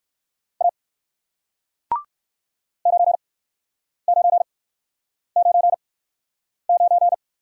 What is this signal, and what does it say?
Morse code, heard as a keyed tone, about 60 wpm, spaced as for 9 wpm, 700 Hz. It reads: A 9999